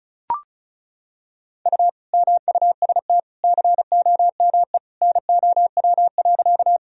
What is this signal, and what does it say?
Morse code, heard as a keyed tone, about 36 wpm, 700 Hz. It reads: U MUST COME NOW.
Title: Sentences from Top 200 Words and Abbreviations 35wpm